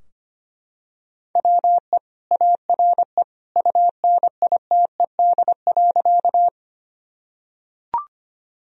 Morse code, heard as a keyed tone, about 25 wpm, 700 Hz. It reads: WE ARE UNITED.
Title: Sentences from Top 1000 Words - Review Part 4 - 25wpm